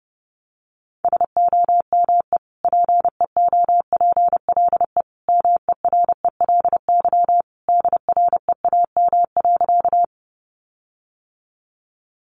Morse code, heard as a keyed tone, about 28 wpm, 700 Hz. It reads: SOME PEOPLE MERELY DREAM.
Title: Sentences from Top 1500 Words 30wpm